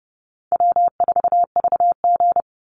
Morse code, heard as a keyed tone, about 30 wpm, 700 Hz. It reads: W4VG
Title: Easy Call Signs - 30wpm